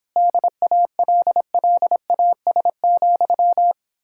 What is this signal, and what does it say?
Morse code, heard as a keyed tone, about 31 wpm, 700 Hz. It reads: DALLAS,